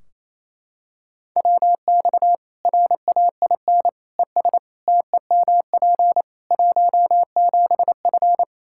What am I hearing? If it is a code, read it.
Morse code, heard as a keyed tone, about 28 wpm, 700 Hz. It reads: WX RAIN ES TEMP 17F
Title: Practice QSO - Exchange 2 - Info - 28wpm